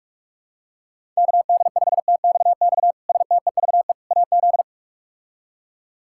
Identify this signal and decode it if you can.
Morse code, heard as a keyed tone, about 42 wpm, 700 Hz. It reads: KD5TXX STEVE AZ
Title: K1USN SST QSO Practice - 45wpm